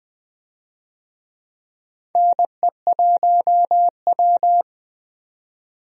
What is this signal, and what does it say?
Morse code, heard as a keyed tone, about 20 wpm, 700 Hz. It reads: NE1W
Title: USA Call Signs - 4 Characters - 20wpm